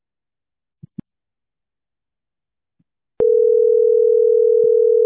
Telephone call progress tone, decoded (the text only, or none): ringback tone